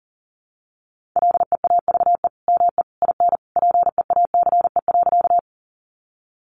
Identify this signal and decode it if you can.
Morse code, heard as a keyed tone, about 40 wpm, 700 Hz. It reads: LEAVE ME IN PEACE.
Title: Sentences from Top 1500 Words 40wpm